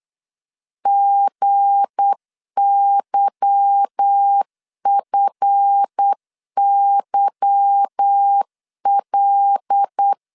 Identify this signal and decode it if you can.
Morse code, heard as a keyed tone, about 8 wpm, 800 Hz. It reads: GYFYL
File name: LEZIONI / 8 YJ45 / 8 Y 8wpm 3245